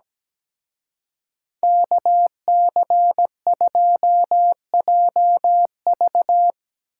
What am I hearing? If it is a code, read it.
Morse code, heard as a keyed tone, about 19 wpm, 700 Hz. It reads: KC2JV
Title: Speed Racing - USA 2x2 Call Signs 17wpm